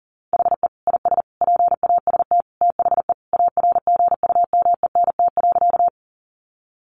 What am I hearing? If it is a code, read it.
Morse code, heard as a keyed tone, about 40 wpm, 700 Hz. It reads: HE IS PAST THE ARGUMENT.
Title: Sentences from Top 1400 Words 40wpm